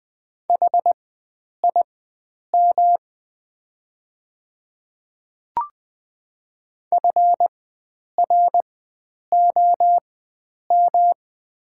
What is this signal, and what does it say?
Morse code, heard as a keyed tone, about 20 wpm, spaced as for 9 wpm, 690 Hz. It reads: HIM FROM
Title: 4x Character Spacing - Top 300 Words - 20wpm